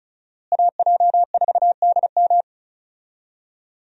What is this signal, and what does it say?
Morse code, heard as a keyed tone, about 36 wpm, 700 Hz. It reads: AJ4DM
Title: USA Call Signs - 5 Characters - Repeated 2x - 35wpm